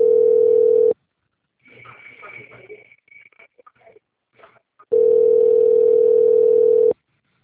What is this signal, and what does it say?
Ringback tone (North America), 440 + 480 Hz.
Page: Limeira - SP